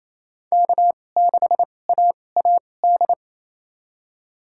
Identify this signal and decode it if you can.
Morse code, heard as a keyed tone, about 28 wpm, spaced as for 20 wpm, 700 Hz. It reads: K6AAD
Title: USA Call Signs - 5 Characters - 2x Character Spacing - 28wpm